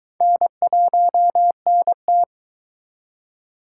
Morse code, heard as a keyed tone, about 23 wpm, 700 Hz. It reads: N1NT